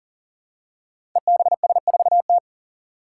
Morse code, heard as a keyed tone, about 40 wpm, 700 Hz. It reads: EBS4T
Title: Five Challenging Letter-Number Combinations 40wpm